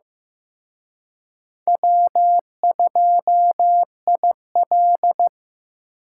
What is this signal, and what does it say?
Morse code, heard as a keyed tone, about 18 wpm, 700 Hz. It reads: W2IL
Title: Speed Racing - USA 1x2 Call Signs 15wpm